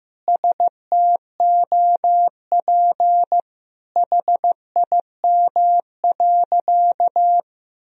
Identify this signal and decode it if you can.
Morse code, heard as a keyed tone, about 15 wpm, 700 Hz. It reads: STOP HIM.
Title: Sentences from Top 1000 Words - Review Part 4 - 15wpm